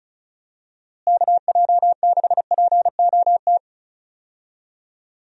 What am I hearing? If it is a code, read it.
Morse code, heard as a keyed tone, about 36 wpm, 700 Hz. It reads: KJ6POT
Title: USA Call Signs - 6 Characters - Repeated 2x - 35wpm